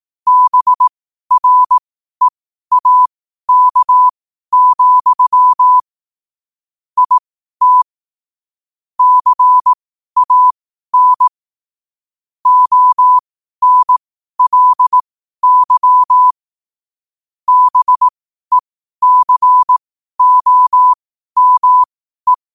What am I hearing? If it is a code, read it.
Morse code, heard as a keyed tone, about 18 wpm, spaced as for 12 wpm, 1000 Hz. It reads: BREAK, IT CAN ONLY BECOME
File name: qotdfiles / 12wpm